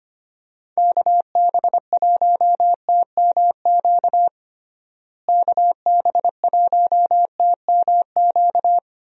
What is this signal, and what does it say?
Morse code, heard as a keyed tone, about 25 wpm, 700 Hz. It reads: KB1TMQ KB1TMQ
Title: USA Call Signs - 6 Characters - Repeated 2x - 25wpm